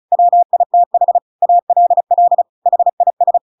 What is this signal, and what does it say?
Morse code, heard as a keyed tone, about 35 wpm, 700 Hz. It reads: WITH ALL HIS